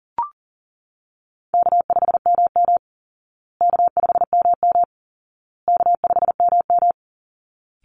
Morse code, heard as a keyed tone, about 40 wpm, 700 Hz. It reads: K5MM K5MM K5MM
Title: Call Signs Repeated 2x - 40wpm